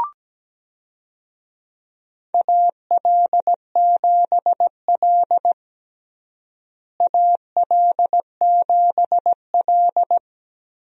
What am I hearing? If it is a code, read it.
Morse code, heard as a keyed tone, about 17 wpm, 700 Hz. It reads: AL7L AL7L
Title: USA Call Signs - 4 Characters - Repeated 3x - 17wpm